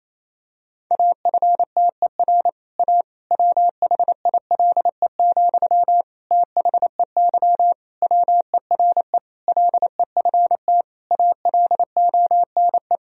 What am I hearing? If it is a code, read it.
Morse code, heard as a keyed tone, about 28 wpm, 700 Hz. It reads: AFTER A WHILE, THEY WERE LEFT ALONE
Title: Sentences from Top 1000 Words - Review Part 1 - 28wpm